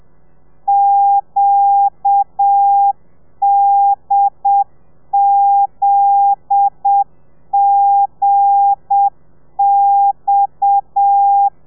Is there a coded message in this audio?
Yes: Morse code from 0.7 to 11.5 s, heard as a keyed tone, about 7 wpm, 800 Hz. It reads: QDZGX